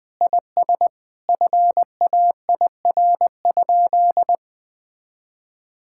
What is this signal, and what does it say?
Morse code, heard as a keyed tone, about 21 wpm, 700 Hz. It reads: IS FAIR?
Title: Sentences with 4-Word Phrases - 20wpm